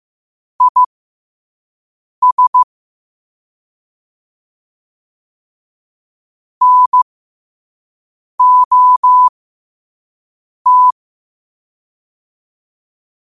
Morse code, heard as a keyed tone, about 15 wpm, spaced as for 5 wpm, 1000 Hz. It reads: IS NOT
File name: qotdfiles / 5wpm